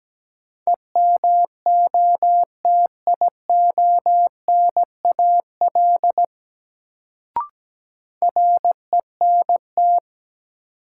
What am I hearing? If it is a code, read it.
Morse code, heard as a keyed tone, about 17 wpm, 700 Hz. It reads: EMOTIONAL RENT
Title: Sentences from Top 1000 Words - Review Part 5 - New Word Drill - 17wpm